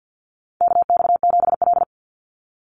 Morse code, heard as a keyed tone, about 50 wpm, 700 Hz. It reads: KX7L
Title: NA SOTA Chaser Callsigns 50wpm